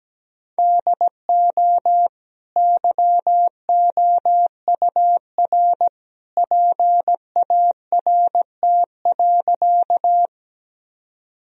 Morse code, heard as a keyed tone, about 17 wpm, 700 Hz. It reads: DO YOUR PART.
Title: Sentences from Top 1000 Words - Review Part 1 - 17wpm